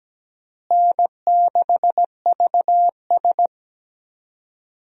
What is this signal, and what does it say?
Morse code, heard as a keyed tone, about 17 wpm, 700 Hz. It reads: N6VS
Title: USA Call Signs - 4 Characters - Repeated 3x - 17wpm